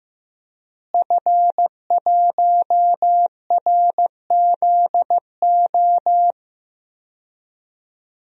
Morse code, heard as a keyed tone, about 15 wpm, 700 Hz. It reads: F1RZO
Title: DXCC-100 Call Signs - 5 Characters - Repeated 2x - 15wpm